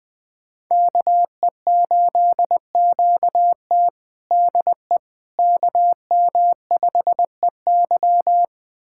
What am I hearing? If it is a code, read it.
Morse code, heard as a keyed tone, about 20 wpm, 700 Hz. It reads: KE8QT DE KM5EY
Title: USA Callsign DE Callsign - 5 Characters - 20wpm